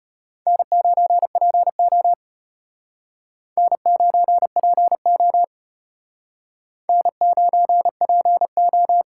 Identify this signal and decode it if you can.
Morse code, heard as a keyed tone, about 34 wpm, 700 Hz. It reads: N9PO N9PO N9PO